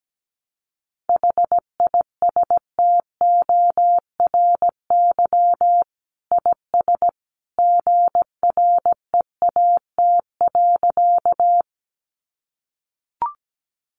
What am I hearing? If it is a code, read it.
Morse code, heard as a keyed tone, about 17 wpm, 700 Hz. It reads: HISTORY IS GREAT.
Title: Sentences from Top 600 Words 17wpm